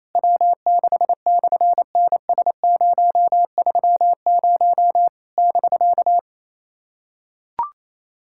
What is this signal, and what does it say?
Morse code, heard as a keyed tone, about 27 wpm, 700 Hz. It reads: W6/NS030 <BK>
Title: SOTA QSO Practice 28wpm